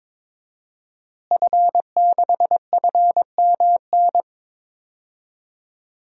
Morse code, heard as a keyed tone, about 21 wpm, 700 Hz. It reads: F6FMN
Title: DXCC-100 Call Signs - 5 Characters - Repeated 3x - 22wpm